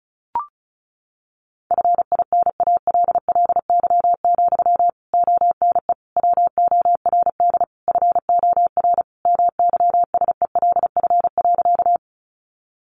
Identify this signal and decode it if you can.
Morse code, heard as a keyed tone, about 34 wpm, 700 Hz. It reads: FINALLY, ONE WORD FOR MYSELF.
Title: Sentences from Top 1300 Words 35wpm